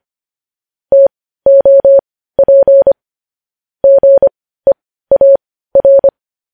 Morse code, heard as a keyed tone, about 25 wpm, spaced as for 15 wpm, 550 Hz. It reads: TOP GEAR
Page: News Headlines in Morse Code at 15 WPM